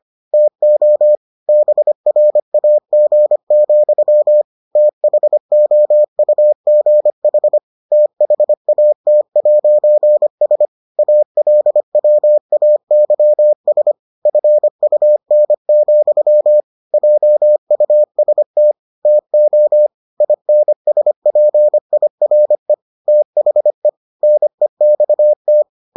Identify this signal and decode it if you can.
Morse code, heard as a keyed tone, about 25 wpm, 600 Hz. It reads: TO BRAG, THOUGH THAT'S ALWAYS FUN, JUST TO INSPIRE THE NEXT